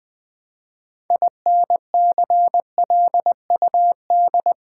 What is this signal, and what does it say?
Morse code, heard as a keyed tone, about 20 wpm, 700 Hz. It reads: INCLUD
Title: Build A Word - 2 Letters At A Time - 8-Letter Words - 20wpm